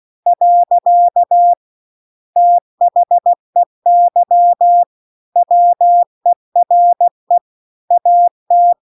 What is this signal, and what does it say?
Morse code, heard as a keyed tone, about 16 wpm, 700 Hz. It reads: . THEY WERE AT
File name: 16WPM